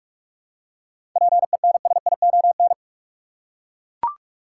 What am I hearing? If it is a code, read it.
Morse code, heard as a keyed tone, about 42 wpm, 700 Hz. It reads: PENSION
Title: Seven Letter Words - 45wpm